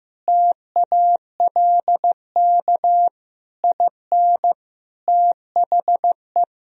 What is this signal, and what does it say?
Morse code, heard as a keyed tone, about 15 wpm, 700 Hz. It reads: TALK IN THE
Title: Sentences from Top 1000 Words - Review Part 4 - 15wpm